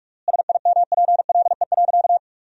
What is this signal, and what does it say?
Morse code, heard as a keyed tone, about 46 wpm, 700 Hz. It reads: SIMPLE.
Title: 3x Word-Spacing - Sentences from Top 500 Words - 45wpm